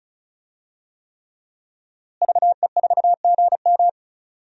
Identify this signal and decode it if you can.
Morse code, heard as a keyed tone, about 36 wpm, 700 Hz. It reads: VE4GM
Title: DXCC-100 Call Signs - 5 Characters - Repeated 3x - 35wpm